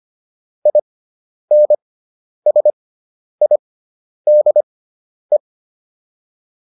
Morse code, heard as a keyed tone, about 25 wpm, spaced as for 10 wpm, 600 Hz. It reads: INSIDE